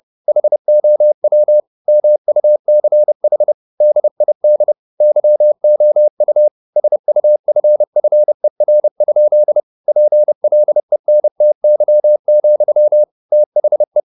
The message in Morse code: HOW MUCH DID YOU SUFFER? PLENTY, THE

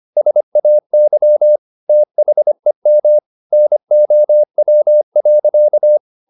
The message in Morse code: SAY THEM NOW.

